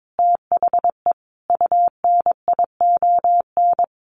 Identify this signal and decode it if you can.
Morse code, heard as a keyed tone, about 22 wpm, 700 Hz. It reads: THE UNION
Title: Sets of 5 Words Encore 22wpm